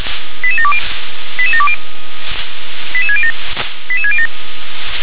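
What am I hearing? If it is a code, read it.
Five-tone selective call (ZVEI), heard as quick runs of five tones, tones of about 70 ms. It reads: 88720 88720 88607 88607
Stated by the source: Received on SDR, ZVEI five-tone call.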